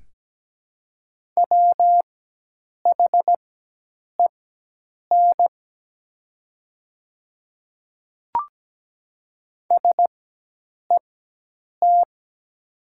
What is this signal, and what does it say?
Morse code, heard as a keyed tone, about 17 wpm, spaced as for 8 wpm, 700 Hz. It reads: WHEN SET
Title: 4x Character Spacing - Top 500 Words - 17wpm